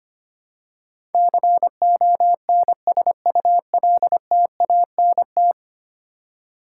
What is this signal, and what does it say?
Morse code, heard as a keyed tone, about 25 wpm, 700 Hz. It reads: CONSULTANT
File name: Ten Letter Words - 25wpm